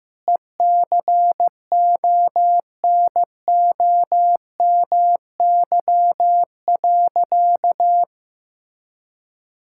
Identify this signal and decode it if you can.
Morse code, heard as a keyed tone, about 15 wpm, 700 Hz. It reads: ECONOMY.
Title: Sentences from Top 1000 Words - Review Part 6 - 15wpm